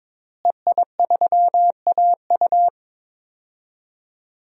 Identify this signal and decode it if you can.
Morse code, heard as a keyed tone, about 22 wpm, 700 Hz. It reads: EI3AU
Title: DXCC-100 Call Signs - 5 Characters - Repeated 2x - 22wpm